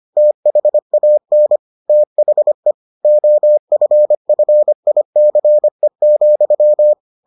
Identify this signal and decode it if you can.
Morse code, heard as a keyed tone, about 25 wpm, 600 Hz. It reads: THAN THE OFFICE,